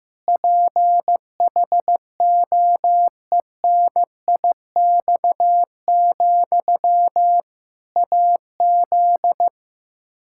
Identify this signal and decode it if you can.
Morse code, heard as a keyed tone, about 15 wpm, 700 Hz. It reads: PHOENIX, AZ
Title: Practice QSO - Exchange 1 - Beginning - 15wpm